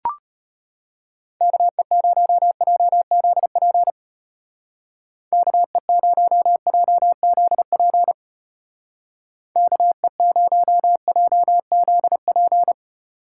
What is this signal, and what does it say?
Morse code, heard as a keyed tone, about 32 wpm, 700 Hz. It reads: KE0JZP KE0JZP KE0JZP